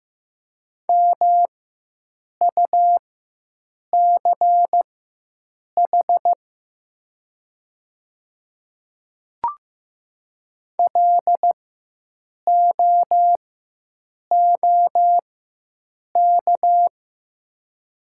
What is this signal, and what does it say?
Morse code, heard as a keyed tone, about 15 wpm, spaced as for 7 wpm, 700 Hz. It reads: MUCH LOOK
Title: 4x Character Spacing - Top 200 Words - 15wpm